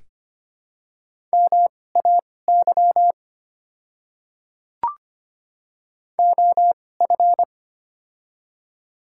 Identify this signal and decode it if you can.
Morse code, heard as a keyed tone, about 25 wpm, spaced as for 18 wpm, 700 Hz. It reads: MAY OF
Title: 2x Character Spacing - Top 100 Words - 25wpm